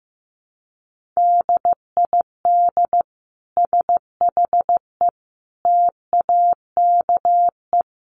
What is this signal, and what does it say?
Morse code, heard as a keyed tone, about 15 wpm, 700 Hz. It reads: DID SHE TAKE
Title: Sentences from Top 1400 Words 15wpm